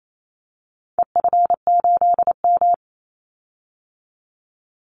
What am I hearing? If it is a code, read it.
Morse code, heard as a keyed tone, about 28 wpm, 700 Hz. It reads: EF8M
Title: Call Signs Repeated 2x - 28wpm